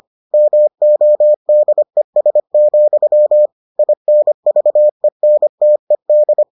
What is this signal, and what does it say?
Morse code, heard as a keyed tone, about 25 wpm, 600 Hz. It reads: MODES, INVENTED